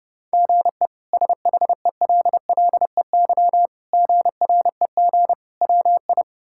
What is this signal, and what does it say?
Morse code, heard as a keyed tone, about 30 wpm, 700 Hz. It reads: GE SHELLEY GREG WI
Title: K1USN SST QSO Practice - 30wpm